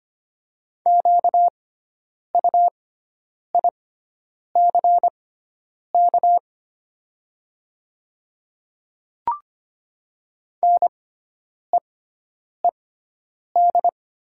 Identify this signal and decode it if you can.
Morse code, heard as a keyed tone, about 25 wpm, spaced as for 9 wpm, 700 Hz. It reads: QUICK NEED
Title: 6x Character Spacing - Top 500 Words - 25wpm